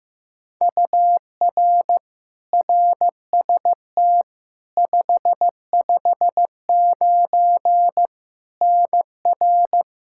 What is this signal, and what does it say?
Morse code, heard as a keyed tone, about 15 wpm, 700 Hz. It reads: UR RST 559 NR
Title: Practice QSO - Exchange 1 - Beginning - 15wpm